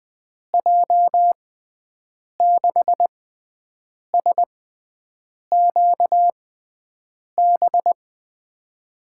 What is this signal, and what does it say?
Morse code, heard as a keyed tone, about 20 wpm, spaced as for 7 wpm, 700 Hz. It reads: J6SQB
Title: USA Call Signs - 6 Characters - 6x Character Spacing - 20wpm